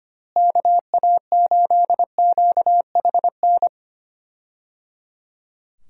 Morse code, heard as a keyed tone, about 25 wpm, 700 Hz. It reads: KA8QHN
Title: USA Call Signs - 6 Characters - Repeated 3x - 25wpm